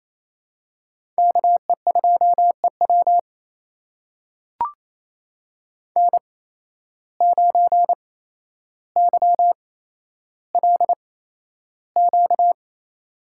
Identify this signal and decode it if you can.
Morse code, heard as a keyed tone, about 28 wpm, spaced as for 12 wpm, 700 Hz. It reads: KE2EW N9YLQ
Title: USA Call Signs - 5 Characters - 8x Character Spacing - 28wpm